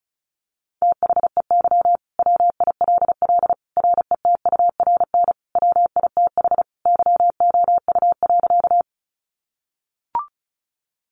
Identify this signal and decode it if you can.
Morse code, heard as a keyed tone, about 34 wpm, 700 Hz. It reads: THEY WILL RETURN WITH YOU.
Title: Sentences from Top 800 Words 35wpm